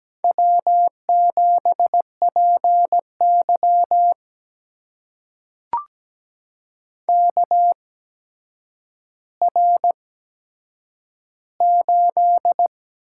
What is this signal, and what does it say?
Morse code, heard as a keyed tone, about 17 wpm, spaced as for 8 wpm, 700 Hz. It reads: W7PY KR8
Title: USA Call Signs - 4 Characters - 8x Character Spacing - 17wpm